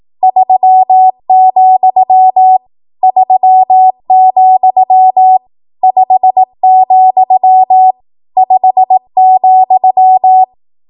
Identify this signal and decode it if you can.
Morse code, heard as a keyed tone, about 18 wpm, 750 Hz. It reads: 3, 3, 5, 5,
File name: Morse / 18WPM